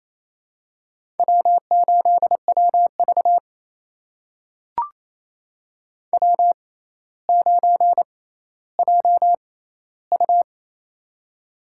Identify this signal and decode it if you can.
Morse code, heard as a keyed tone, about 28 wpm, spaced as for 14 wpm, 700 Hz. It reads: W8WV W9JU
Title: USA Call Signs - 4 Characters - 6x Character Spacing - 28wpm